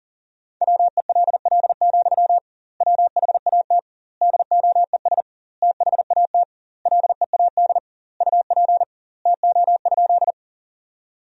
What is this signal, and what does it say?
Morse code, heard as a keyed tone, about 38 wpm, 700 Hz. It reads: WELL, WHAT DOES THAT LEAD UP TO?
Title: 2x Word-Spacing - Sentences from Top 500 Words - 40wpm